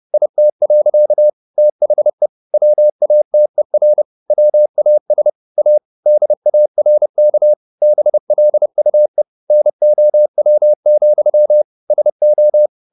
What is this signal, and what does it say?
Morse code, heard as a keyed tone, about 30 wpm, 600 Hz. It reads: IT. THE WATER WAS A DARK BLUE NOW, SO
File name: the old man and the sea / the old man and the sea 1 30wpm